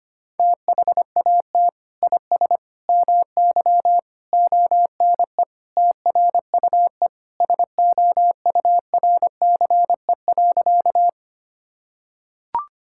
Morse code, heard as a keyed tone, about 25 wpm, 700 Hz. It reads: THAT IS MY ONE TRUE SOURCE.
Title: Sentences from Top 1000 Words - Review Part 5 - 25wpm